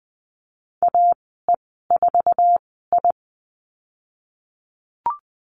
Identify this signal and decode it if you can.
Morse code, heard as a keyed tone, about 20 wpm, spaced as for 15 wpm, 700 Hz. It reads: AE4I
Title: 055 Lesson 10g - Letter r - Call Signs 20wpm